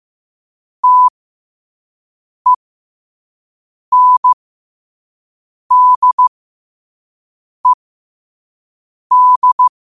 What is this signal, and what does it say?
Morse code, heard as a keyed tone, about 15 wpm, spaced as for 5 wpm, 1000 Hz. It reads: TENDED
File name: qotdfiles / 5wpm